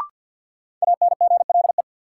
Morse code, heard as a keyed tone, about 52 wpm, 700 Hz. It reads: ANGLE
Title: Sentences from Top 1500 Words - New Word Drill - 50wpm